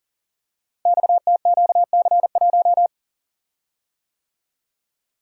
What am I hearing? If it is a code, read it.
Morse code, heard as a keyed tone, about 42 wpm, 700 Hz. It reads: XTQC1
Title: Code Groups - 5 Letter:Numbers - 40wpm